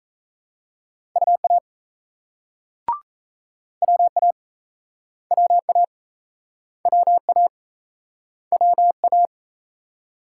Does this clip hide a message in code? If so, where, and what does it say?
Yes: Morse code from 1.2 to 9.3 s, heard as a keyed tone, about 38 wpm, 700 Hz. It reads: UA WA WA WA WA